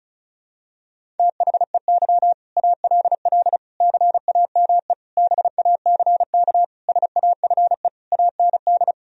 Morse code, heard as a keyed tone, about 34 wpm, 700 Hz. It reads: THEY ALL CAME BACK SAFE AND
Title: Sentences from Top 1000 Words - Review Part 1 - 35wpm